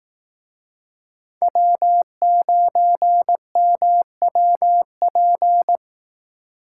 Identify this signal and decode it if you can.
Morse code, heard as a keyed tone, about 18 wpm, 700 Hz. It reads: W9MWP
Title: Speed Racing - USA 1x3 Call Signs 15wpm